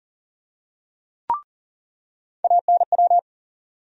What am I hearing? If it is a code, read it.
Morse code, heard as a keyed tone, about 40 wpm, 700 Hz. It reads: ANW
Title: USA Call Sign Postfixes - 3 Letters 40wpm